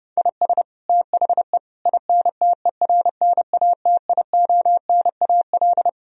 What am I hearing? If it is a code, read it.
Morse code, heard as a keyed tone, about 30 wpm, 700 Hz. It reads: IS THE INTERNATIONAL
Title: Sentences from Top 1000 Words - Review Part 4 - 30wpm